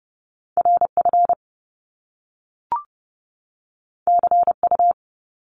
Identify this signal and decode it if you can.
Morse code, heard as a keyed tone, about 30 wpm, 700 Hz. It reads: RF CU
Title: Top Two Letter Combinations in Words 30wpm